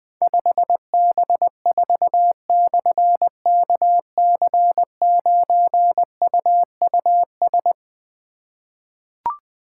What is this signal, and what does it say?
Morse code, heard as a keyed tone, about 20 wpm, 700 Hz. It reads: 5B4/KC9UUS
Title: Difficult Call Signs 20wpm